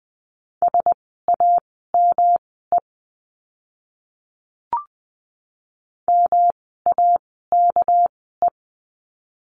Morse code, heard as a keyed tone, about 20 wpm, spaced as for 15 wpm, 700 Hz. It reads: SAME MAKE
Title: 237 Lesson 38f - Letter q - Words 20wpm